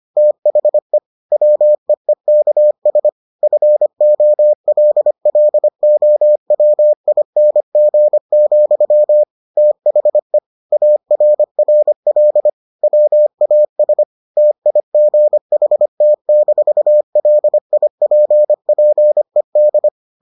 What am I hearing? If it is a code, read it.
Morse code, heard as a keyed tone, about 25 wpm, 600 Hz. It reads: THE WEEKS FOLLOWING, THE ARRL WAS TIGHT-LIPPED